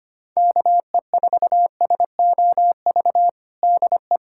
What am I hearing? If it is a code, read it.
Morse code, heard as a keyed tone, about 25 wpm, 700 Hz. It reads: KE4SOV DE
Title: USA Callsign DE Callsign - 6 Characters - 25wpm